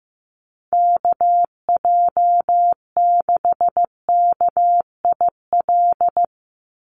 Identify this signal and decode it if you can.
Morse code, heard as a keyed tone, about 15 wpm, 700 Hz. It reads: KJ6KIL
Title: USA Call Signs 15wpm